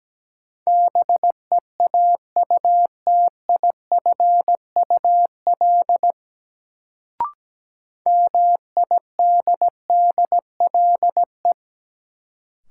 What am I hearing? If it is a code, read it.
Morse code, heard as a keyed tone, about 17 wpm, 700 Hz. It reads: BEAUTIFUL MIDDLE
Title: Sentences from Top 900 Words - New Word Drill - 17wpm